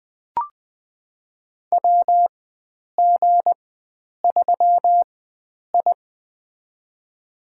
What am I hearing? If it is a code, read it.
Morse code, heard as a keyed tone, about 21 wpm, spaced as for 9 wpm, 700 Hz. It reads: WG3I
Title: USA Call Signs - 4 Characters - 4x Character Spacing - 20wpm